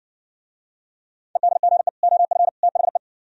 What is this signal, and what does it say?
Morse code, heard as a keyed tone, about 69 wpm, 700 Hz. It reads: EDGE OF THE